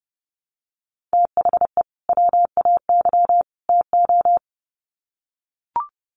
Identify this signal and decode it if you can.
Morse code, heard as a keyed tone, about 30 wpm, 700 Hz. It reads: THE WAY TO
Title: Sets of 3 Words 30wpm